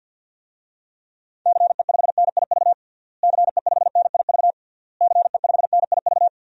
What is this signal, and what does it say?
Morse code, heard as a keyed tone, about 50 wpm, 700 Hz. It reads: CE5NIV CE5NIV CE5NIV
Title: DXCC-100 Call Signs - 6 Characters - Repeated 3x - 50wpm